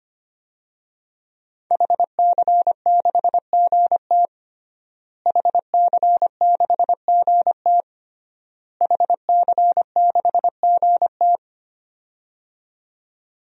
Morse code, heard as a keyed tone, about 25 wpm, 700 Hz. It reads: HC6GT HC6GT HC6GT
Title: DXCC-100 Call Signs - 5 Characters - Repeated 3x - 25wpm